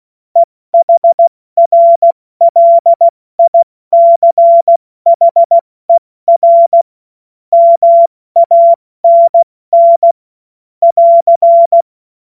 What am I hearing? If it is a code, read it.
Morse code, heard as a keyed tone, about 16 wpm, 680 Hz. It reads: EHRLICHER MANN <AR>
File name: Morsekurs / Witze / Witz 23. August 2022 70 F80